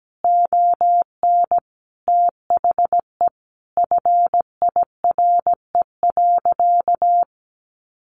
Morse code, heard as a keyed tone, about 17 wpm, 700 Hz. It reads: ON THE FIRE.
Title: Sentences from Top 1400 Words 17wpm